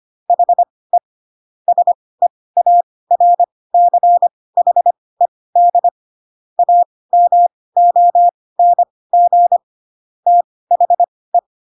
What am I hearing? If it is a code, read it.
Morse code, heard as a keyed tone, about 25 wpm, spaced as for 18 wpm, 700 Hz. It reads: HE SEARCHED AMONG THE
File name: Pinocchio / Pinocchio ch1 12 25 ik1hge